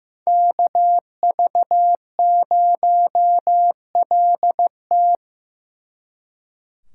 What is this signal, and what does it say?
Morse code, heard as a keyed tone, about 18 wpm, 700 Hz. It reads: KV0LT
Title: Speed Racing - USA Call Signs 15wpm